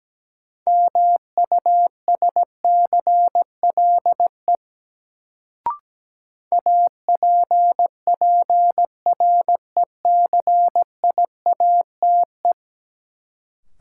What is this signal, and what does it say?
Morse code, heard as a keyed tone, about 17 wpm, 700 Hz. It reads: MUSCLE APPRECIATE
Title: Sentences from Top 1300 Words - New Word Drill Alt - 17wpm